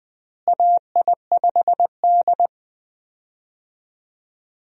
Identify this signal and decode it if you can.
Morse code, heard as a keyed tone, about 20 wpm, 700 Hz. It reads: AI5D
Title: USA Call Signs - 4 Characters - Repeated 3x - 20wpm